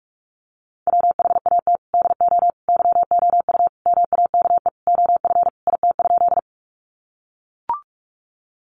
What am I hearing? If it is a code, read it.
Morse code, heard as a keyed tone, about 44 wpm, 700 Hz. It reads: WHAT DO YOU MAKE OF IT?
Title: Sentences from Top 100 Words 45wpm